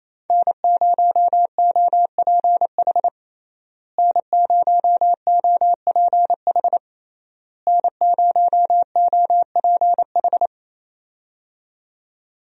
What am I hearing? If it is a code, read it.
Morse code, heard as a keyed tone, about 28 wpm, 700 Hz. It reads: N0OPH N0OPH N0OPH
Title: USA Call Signs - 5 Characters - Repeated 3x - 28wpm